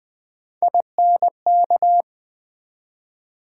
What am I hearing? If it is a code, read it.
Morse code, heard as a keyed tone, about 20 wpm, 700 Hz. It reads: INK
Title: Build A Word - 1 Letter At A Time - 3-Letter Words - 20wpm